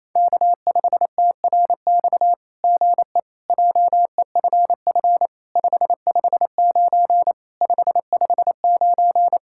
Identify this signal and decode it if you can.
Morse code, heard as a keyed tone, about 27 wpm, 700 Hz. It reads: K5TRX GE JEFF 559 559
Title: SOTA QSO Practice 28wpm